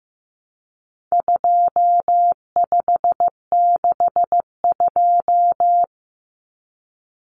Morse code, heard as a keyed tone, about 15 wpm, 700 Hz. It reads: 2562
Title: Common Cut Numbers - 4-Digit 15wpm